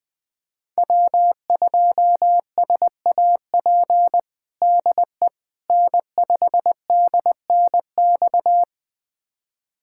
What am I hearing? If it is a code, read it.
Morse code, heard as a keyed tone, about 20 wpm, 700 Hz. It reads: W2SAP DE N5DNX
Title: USA Callsign DE Callsign - 5 Characters - 20wpm